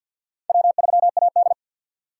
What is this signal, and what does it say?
Morse code, heard as a keyed tone, about 50 wpm, 700 Hz. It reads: W3AD
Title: USA Call Signs - 4 Characters - 3x Character Spacing - 50wpm